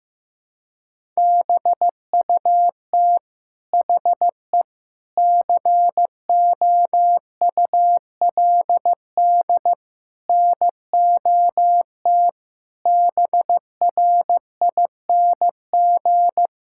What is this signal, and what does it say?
Morse code, heard as a keyed tone, about 15 wpm, 700 Hz. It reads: BUT HE COULD NOT BRING
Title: Sentences from Top 1600 Words 15wpm